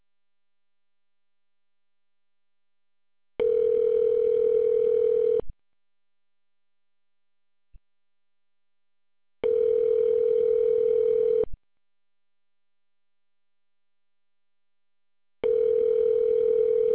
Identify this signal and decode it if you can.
Ringback tone (North America), 440 + 480 Hz.